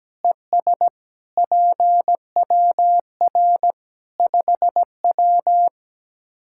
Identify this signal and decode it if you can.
Morse code, heard as a keyed tone, about 17 wpm, 700 Hz. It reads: ES PWR 5W
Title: Practice QSO - Exchange 2 - Info - 17wpm